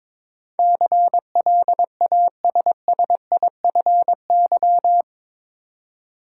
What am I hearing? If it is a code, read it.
Morse code, heard as a keyed tone, about 22 wpm, 700 Hz. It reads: CLASSIFY
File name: practice / 22wpm / Eight Letter Words - 22wpm